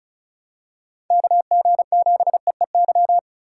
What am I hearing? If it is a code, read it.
Morse code, heard as a keyed tone, about 34 wpm, 700 Hz. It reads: KG7EEY
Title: USA Call Signs - 6 Characters - 2x Character Spacing - 35wpm